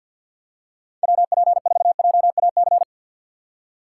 Morse code, heard as a keyed tone, about 53 wpm, 700 Hz. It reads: WP4JAC
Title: USA Call Signs - 6 Characters - 6x Character Spacing - 50wpm